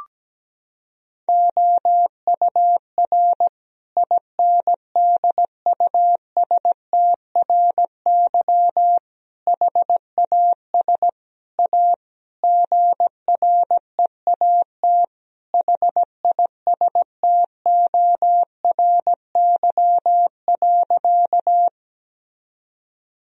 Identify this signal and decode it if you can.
Morse code, heard as a keyed tone, about 17 wpm, 700 Hz. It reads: OUR INDUSTRY HAS A GREAT HISTORY.
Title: Sentences from Top 1000 Words - Review Part 4 - 17wpm